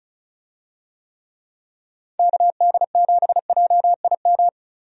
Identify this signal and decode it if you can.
Morse code, heard as a keyed tone, about 36 wpm, 700 Hz. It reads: KD7JIM
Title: USA Call Signs - 6 Characters - Repeated 3x - 35wpm